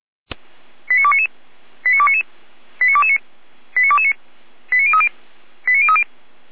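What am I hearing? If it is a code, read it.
Five-tone selective call (ZVEI), heard as quick runs of five tones, tones of about 70 ms. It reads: 89299 89299 89229 89229 89030 89030
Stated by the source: ZVEI five-tone call 89075 received on an SDR.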